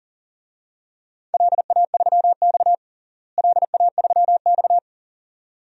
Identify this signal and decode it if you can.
Morse code, heard as a keyed tone, about 40 wpm, 700 Hz. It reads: LA3X LA3X
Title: DXCC-100 Call Signs - 4 Characters - Repeated 2x - 40wpm